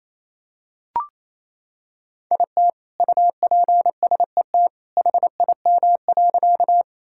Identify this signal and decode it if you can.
Morse code, heard as a keyed tone, about 28 wpm, 700 Hz. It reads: IT UPSET HIM.
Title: Sentences from Top 1000 Words - Review Part 5 - 28wpm